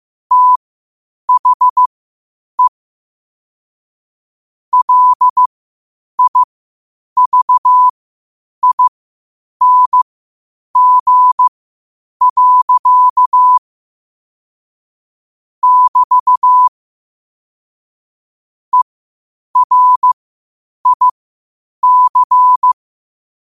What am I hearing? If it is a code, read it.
Morse code, heard as a keyed tone, about 15 wpm, spaced as for 8 wpm, 1000 Hz. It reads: THE LIVING. <BT> ERIC
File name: qotdfiles / 8wpm